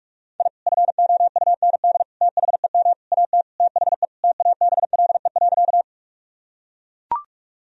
Morse code, heard as a keyed tone, about 42 wpm, 700 Hz. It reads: I FOUND THEM AT THE TABLE.
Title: Sentences from Top 1000 Words - Review Part 2 - 45wpm